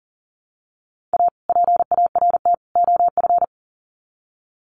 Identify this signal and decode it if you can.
Morse code, heard as a keyed tone, about 40 wpm, 700 Hz. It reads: A PART OF
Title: Sets of 3 Words 40wpm